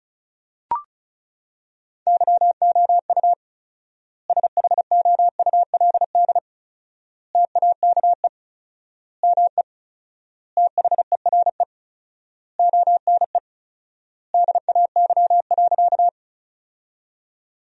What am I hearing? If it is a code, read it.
Morse code, heard as a keyed tone, about 36 wpm, 700 Hz. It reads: YOU SHOULD TAKE ME THERE ONE DAY.
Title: 4x Word-Spacing - Sentences from Top 300 Words - 35wpm